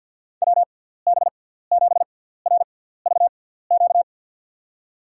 Morse code, heard as a keyed tone, about 53 wpm, spaced as for 17 wpm, 700 Hz. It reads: WB7RVQ
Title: USA Call Signs - 6 Characters - 6x Character Spacing - 50wpm